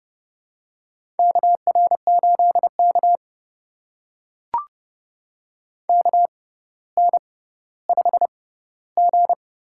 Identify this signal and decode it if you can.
Morse code, heard as a keyed tone, about 30 wpm, spaced as for 15 wpm, 700 Hz. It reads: KR8K KN5G
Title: USA Call Signs - 4 Characters - 6x Character Spacing - 30wpm